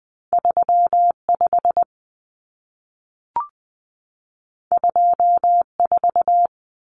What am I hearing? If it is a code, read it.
Morse code, heard as a keyed tone, about 20 wpm, 700 Hz. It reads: 35 24
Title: Common Cut Numbers - 2-Digit 20wpm